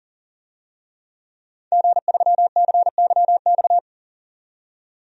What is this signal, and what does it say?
Morse code, heard as a keyed tone, about 40 wpm, 700 Hz. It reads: G3CYX
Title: DXCC-100 Call Signs - 5 Characters - 40wpm